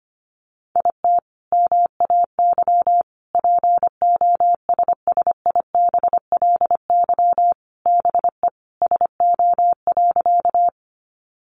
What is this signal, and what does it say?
Morse code, heard as a keyed tone, about 25 wpm, 700 Hz. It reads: IT MAY POSSIBLY BE SO.
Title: Sentences from Top 1300 Words 25wpm